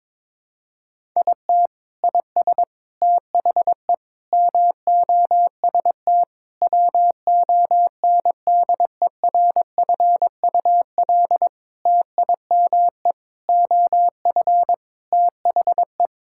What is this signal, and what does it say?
Morse code, heard as a keyed tone, about 22 wpm, 700 Hz. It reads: IT IS THE MOST WONDERFUL TIME OF THE
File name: practice / 22wpm / Sentences with 8-Word Phrases - 22wpm